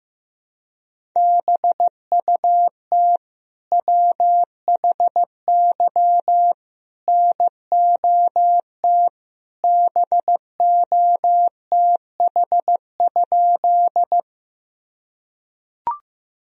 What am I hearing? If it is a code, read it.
Morse code, heard as a keyed tone, about 15 wpm, 700 Hz. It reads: BUT WHY NOT BOTH?
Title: Sentences from Top 1000 Words - Review Part 1 - 15wpm